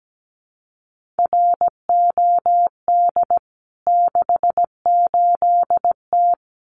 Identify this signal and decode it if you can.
Morse code, heard as a keyed tone, about 17 wpm, 700 Hz. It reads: ROD 68T
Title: CWT Exchange 17wpm